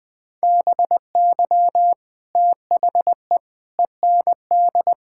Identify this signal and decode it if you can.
Morse code, heard as a keyed tone, about 20 wpm, 700 Hz. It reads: BY THE END
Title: Sentences with 5-Word Phrases - 20wpm